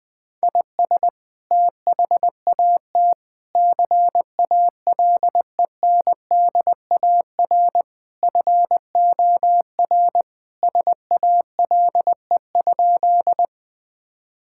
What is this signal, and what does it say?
Morse code, heard as a keyed tone, about 20 wpm, 700 Hz. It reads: IS THAT CALENDAR FOR SALE?
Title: Sentences from Top 1600 Words 20wpm